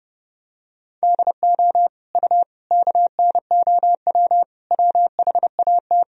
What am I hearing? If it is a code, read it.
Morse code, heard as a keyed tone, about 31 wpm, 700 Hz. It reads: DO U KNOW WHAT
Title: Sentences from Top 100 Words and Abbreviations 30wpm